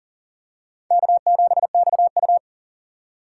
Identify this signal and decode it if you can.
Morse code, heard as a keyed tone, about 39 wpm, 700 Hz. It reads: K7XU
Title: USA Call Signs - 4 Characters - 2x Character Spacing - 40wpm